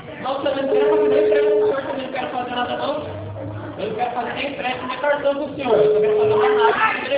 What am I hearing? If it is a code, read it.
Ringback tone (Europe, 425 Hz), with speech.